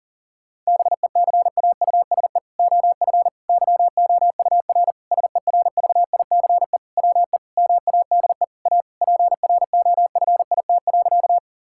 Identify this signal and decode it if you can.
Morse code, heard as a keyed tone, about 37 wpm, 700 Hz. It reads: BECAUSE OF YOUR SERVICE WE MADE A PROFIT.
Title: Sentences from Top 1000 Words - Review Part 5 - 40wpm